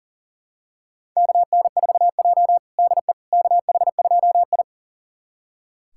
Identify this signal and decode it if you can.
Morse code, heard as a keyed tone, about 40 wpm, 700 Hz. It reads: KN4J DE KH2I
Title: USA Callsign DE Callsign - 4 Characters - 40wpm